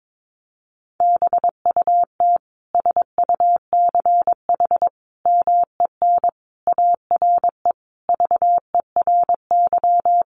Morse code, heard as a keyed tone, about 22 wpm, 700 Hz. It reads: BUT SUCH MEN ARE VERY
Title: Sentences from Top 1100 Words 22wpm